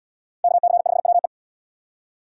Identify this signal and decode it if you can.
Morse code, heard as a keyed tone, about 77 wpm, 700 Hz. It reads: KC5WE